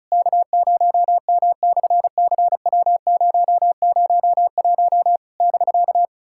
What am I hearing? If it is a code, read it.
Morse code, heard as a keyed tone, about 33 wpm, 700 Hz. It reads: K0M/CW001 <BK>
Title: SOTA QSO Practice 35wpm